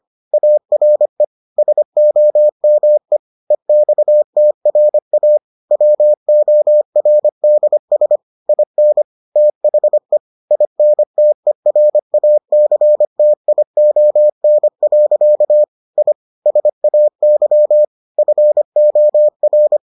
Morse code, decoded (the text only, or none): ARE SOME EXTRA WORDS IN THE INTERACTION. I SAY FOR